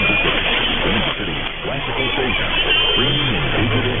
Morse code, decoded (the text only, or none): none